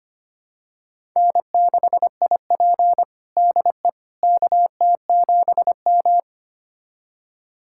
Morse code, heard as a keyed tone, about 25 wpm, 700 Hz. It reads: N6IP DE KT7M
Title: USA Callsign DE Callsign - 4 Characters - 25wpm